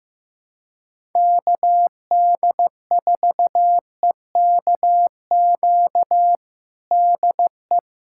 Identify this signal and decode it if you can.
Morse code, heard as a keyed tone, about 15 wpm, 700 Hz. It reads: KD4EKQ DE
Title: USA Callsign DE Callsign - 6 Characters - 15wpm